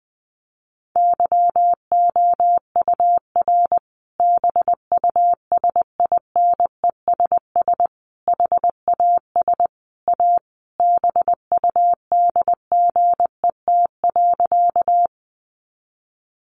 Morse code, heard as a keyed tone, about 20 wpm, 700 Hz. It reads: YOUR BUSINESS HAS A BUDGET.
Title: Sentences from Top 1200 Words 20wpm